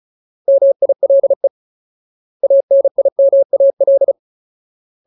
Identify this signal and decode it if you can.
Morse code, heard as a keyed tone, about 34 wpm, 550 Hz. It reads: MILE ANIMAL
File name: Ultimate Word List / 35WPM Ultimate